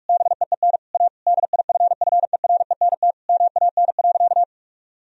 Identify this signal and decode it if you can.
Morse code, heard as a keyed tone, about 46 wpm, 700 Hz. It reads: BEEN A DIFFERENT MAN.
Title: Sentences from Top 1000 Words - Review Part 3 - 45wpm